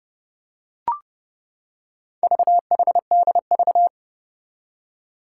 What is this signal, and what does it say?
Morse code, heard as a keyed tone, about 28 wpm, 700 Hz. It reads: VHDV
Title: Four Challenging Letter-Number Combinations 30wpm